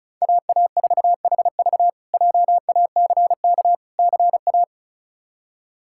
Morse code, heard as a keyed tone, about 34 wpm, 700 Hz. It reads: AA4HV JACK CA
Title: K1USN SST QSO Practice - 35wpm